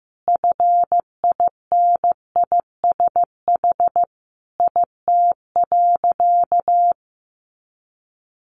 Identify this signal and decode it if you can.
Morse code, heard as a keyed tone, about 15 wpm, 700 Hz. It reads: FINISH IT.
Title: Additional Easy Phrases 15wpm